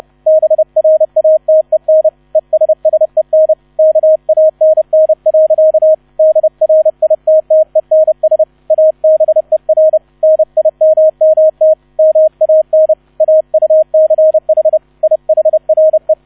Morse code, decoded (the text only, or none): BRATEN ESSEN KANN. DRITTENS ABER NIMMT MAN AUCH IHRE